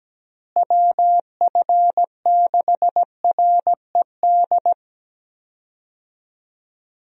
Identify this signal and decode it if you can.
Morse code, heard as a keyed tone, about 17 wpm, 700 Hz. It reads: WF6RED
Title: USA Call Signs - 6 Characters - Repeated 3x - 17wpm